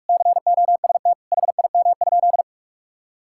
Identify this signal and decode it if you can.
Morse code, heard as a keyed tone, about 41 wpm, 700 Hz. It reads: COST HIM?
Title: Sentences from Top 1000 Words - Review Part 5 - 45wpm